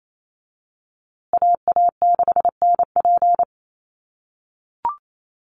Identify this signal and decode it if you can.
Morse code, heard as a keyed tone, about 28 wpm, 700 Hz. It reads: AA6NP
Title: USA Call Signs Repeated 3x 28wpm